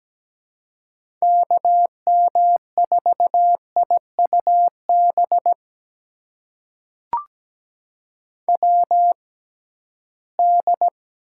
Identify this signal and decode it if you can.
Morse code, heard as a keyed tone, about 17 wpm, spaced as for 13 wpm, 700 Hz. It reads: KM4IUB WD
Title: USA Call Signs - 6 Characters - 6x Character Spacing - 17wpm